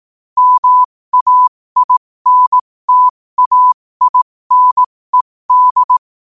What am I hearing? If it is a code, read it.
Morse code, heard as a keyed tone, about 18 wpm, spaced as for 15 wpm, 1000 Hz. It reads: MAINTAINED
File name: qotdfiles / 15wpm